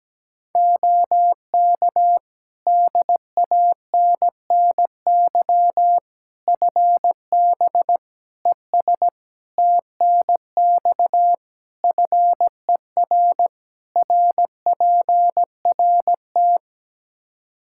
Morse code, heard as a keyed tone, about 17 wpm, 700 Hz. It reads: OK DANNY FB ES TNX FER RPRT
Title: Practice QSO - Exchange 2 - Info - 17wpm